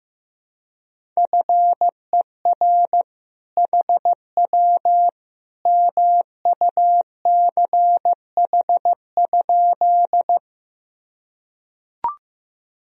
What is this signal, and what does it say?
Morse code, heard as a keyed tone, about 15 wpm, 700 Hz. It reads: FER HW MUCH?
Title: Sentences from Top 200 Words and Abbreviations 15wpm